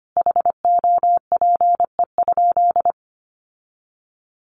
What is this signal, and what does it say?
Morse code, heard as a keyed tone, about 25 wpm, 700 Hz. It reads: HOPE?
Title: Sentences from Top 1000 Words 25wpm